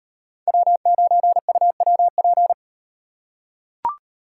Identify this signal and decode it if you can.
Morse code, heard as a keyed tone, about 38 wpm, 700 Hz. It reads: W9UWP